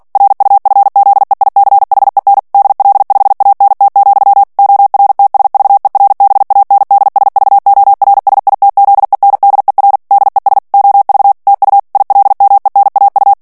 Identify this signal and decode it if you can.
Morse code, heard as a keyed tone, about 48 wpm, 800 Hz. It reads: RUFZEICHEN DL5ANT, ORTSVERBANDSVORSITZENDER DES OV TU ILMENAU
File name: morse mp3 / ALLEMAND texte clair 2 vitesse 45 mots minute